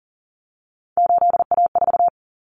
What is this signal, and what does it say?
Morse code, heard as a keyed tone, about 40 wpm, 700 Hz. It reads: 8A4
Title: Common Cut Numbers - 3-Digit 40wpm